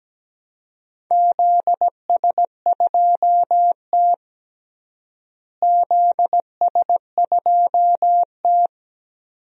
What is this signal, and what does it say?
Morse code, heard as a keyed tone, about 17 wpm, 700 Hz. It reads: ZS2T ZS2T
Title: DXCC-100 Call Signs - 4 Characters - Repeated 2x - 17wpm